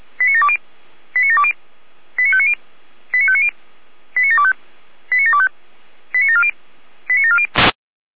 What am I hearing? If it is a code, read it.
Five-tone selective call (ZVEI), heard as quick runs of five tones, tones of about 70 ms. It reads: 89720 89720 89590 89590 89725 89725 89740 89740
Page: /455/SDR/ZVEI/